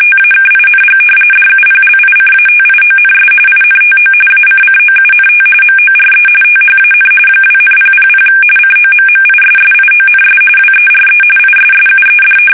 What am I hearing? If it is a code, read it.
RTTY, about 45.45 baud. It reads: TAKING DUPLICATIONS INTO ACCOUNT MORE THAN 2,000 COMMISSIONS WERE ISSUED
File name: fsk850 / usnavy1